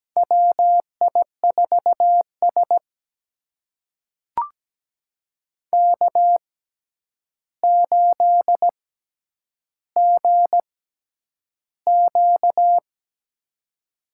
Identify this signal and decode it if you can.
Morse code, heard as a keyed tone, about 17 wpm, spaced as for 9 wpm, 700 Hz. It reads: WI4S K8GQ
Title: USA Call Signs - 4 Characters - 6x Character Spacing - 17wpm